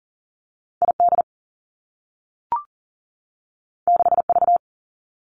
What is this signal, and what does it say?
Morse code, heard as a keyed tone, about 42 wpm, 700 Hz. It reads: I D 6 V
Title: Two Challenging Letter-Number Combinations 40wpm